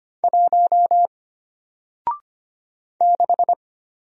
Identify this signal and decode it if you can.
Morse code, heard as a keyed tone, about 25 wpm, 700 Hz. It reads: 1 6
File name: Single-Letter-Number-25wpm